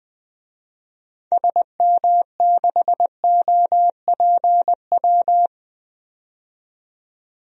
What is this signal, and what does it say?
Morse code, heard as a keyed tone, about 20 wpm, 700 Hz. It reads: SM6OPW
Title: DXCC-100 Call Signs - 6 Characters - 20wpm